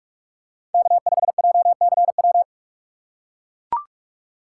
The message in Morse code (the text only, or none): K5JCW